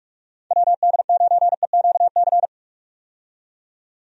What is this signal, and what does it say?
Morse code, heard as a keyed tone, about 46 wpm, 700 Hz. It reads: WD9EQC
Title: USA Call Signs - 6 Characters - 45wpm